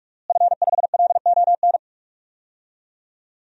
Morse code, heard as a keyed tone, about 41 wpm, 700 Hz. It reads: F5LON
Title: DXCC-100 Call Signs - 5 Characters - Repeated 2x - 45wpm